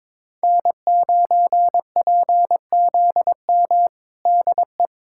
Morse code, heard as a keyed tone, about 22 wpm, 700 Hz. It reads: N9PZM DE
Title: USA Callsign DE Callsign - 5 Characters - 22wpm